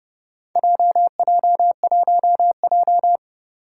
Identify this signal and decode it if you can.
Morse code, heard as a keyed tone, about 30 wpm, 700 Hz. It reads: JJ1J
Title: Four Challenging Letter-Number Combinations 30wpm